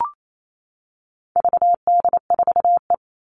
Morse code, heard as a keyed tone, about 28 wpm, 700 Hz. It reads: VD4E
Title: All Cut Numbers - 4-Digit 28wpm